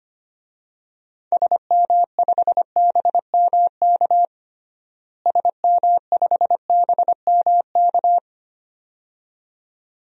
Morse code, heard as a keyed tone, about 25 wpm, 700 Hz. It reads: SM5BMK SM5BMK
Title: DXCC-100 Call Signs - 6 Characters - Repeated 2x - 25wpm